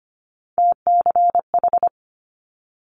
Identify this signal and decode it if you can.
Morse code, heard as a keyed tone, about 25 wpm, 700 Hz. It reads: TCH
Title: Multi-letter Phonemes Quick Fire 25wpm